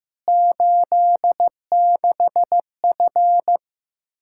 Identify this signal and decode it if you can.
Morse code, heard as a keyed tone, about 15 wpm, 700 Hz. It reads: 86F
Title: K1USN SST QSO Practice 15wpm